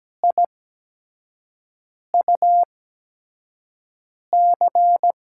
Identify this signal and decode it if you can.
Morse code, heard as a keyed tone, about 17 wpm, spaced as for 5 wpm, 700 Hz. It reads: IUC
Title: USA Call Signs - 6 Characters - 8x Character Spacing - 17wpm